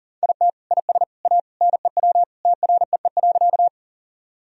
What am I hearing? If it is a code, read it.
Morse code, heard as a keyed tone, about 37 wpm, 700 Hz. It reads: IT IS A NEW TREE.
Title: Sentences from Top 1000 Words - Review Part 1 - 40wpm